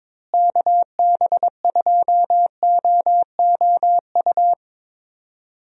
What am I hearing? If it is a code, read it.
Morse code, heard as a keyed tone, about 22 wpm, 700 Hz. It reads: KB2OOU
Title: USA Call Signs - 6 Characters - Repeated 2x - 22wpm